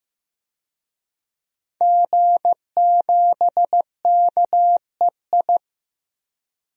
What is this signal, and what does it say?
Morse code, heard as a keyed tone, about 15 wpm, 700 Hz. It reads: G7KEI
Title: DXCC-100 Call Signs - 5 Characters - Repeated 3x - 15wpm